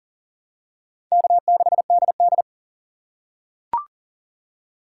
Morse code, heard as a keyed tone, about 40 wpm, 700 Hz. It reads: K6DD
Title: USA Call Signs - 4 Characters - 2x Character Spacing - 40wpm